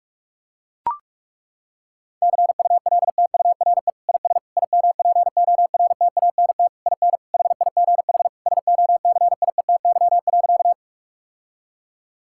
Morse code, heard as a keyed tone, about 41 wpm, 700 Hz. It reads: CULTURE IS IMPORTANT IN HIGH SOCIETY.
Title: Sentences from Top 1000 Words - Review Part 6 - 45wpm